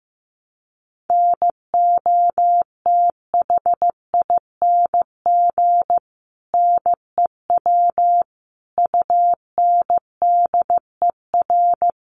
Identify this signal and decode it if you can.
Morse code, heard as a keyed tone, about 15 wpm, 700 Hz. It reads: NOTHING NEW UNDER
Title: Sentences from Top 500 Words 15wpm